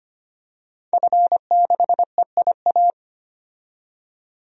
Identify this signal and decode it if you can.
Morse code, heard as a keyed tone, about 25 wpm, 700 Hz. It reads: F6EIA
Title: DXCC-100 Call Signs - 5 Characters - 25wpm